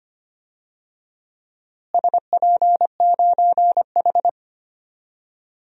Morse code, heard as a keyed tone, about 25 wpm, 700 Hz. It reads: SP9H
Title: DXCC-100 Call Signs - 4 Characters - Repeated 3x - 25wpm